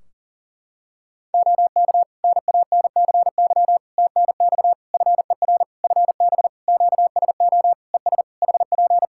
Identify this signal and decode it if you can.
Morse code, heard as a keyed tone, about 40 wpm, 700 Hz. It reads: OK NANCY TNX FER FB QSO ES HP
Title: Practice QSO - Exchange 4 - Ending - 40wpm